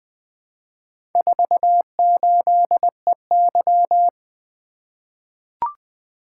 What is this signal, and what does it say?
Morse code, heard as a keyed tone, about 20 wpm, 700 Hz. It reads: 48EY
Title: Four Letter-Number Combinations - 20wpm